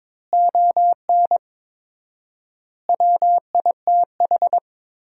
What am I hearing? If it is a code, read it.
Morse code, heard as a keyed tone, about 22 wpm, 700 Hz. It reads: ON WITH
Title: 4x Word-Spacing - Sentences from Top 300 Words - 22wpm